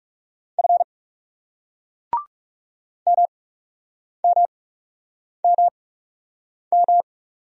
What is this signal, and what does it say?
Morse code, heard as a keyed tone, about 40 wpm, spaced as for 8 wpm, 700 Hz. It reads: F MMMM